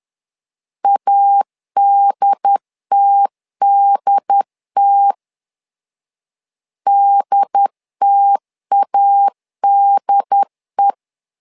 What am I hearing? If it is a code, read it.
Morse code, heard as a keyed tone, about 10 wpm, 800 Hz. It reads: ADTDT DTADE
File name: LEZIONI / 2 UHDN / 2 UHD 10wpm 1826